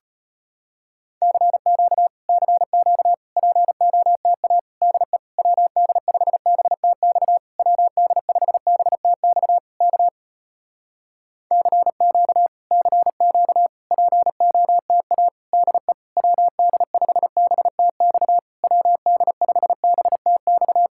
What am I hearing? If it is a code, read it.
Morse code, heard as a keyed tone, about 33 wpm, 700 Hz. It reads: CQ CQ POTA DE WD5BTX WD5BTX K CQ CQ POTA DE WD5BTX WD5BTX